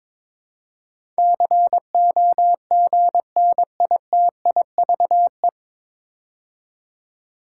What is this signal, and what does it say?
Morse code, heard as a keyed tone, about 22 wpm, 700 Hz. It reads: COGNITIVE
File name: 9 Letter Words - First 3 Letters Given - 22wpm